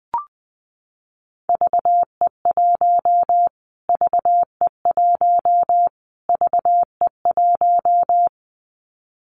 Morse code, heard as a keyed tone, about 20 wpm, 700 Hz. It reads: VE1 VE1 VE1
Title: Callsign Prefixes for Top 25 DXCC - Repeated 3x - 20wpm